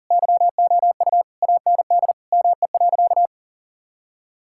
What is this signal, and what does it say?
Morse code, heard as a keyed tone, about 40 wpm, 700 Hz. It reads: YOU AND ME.
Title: Sentences from Top 1600 Words 40wpm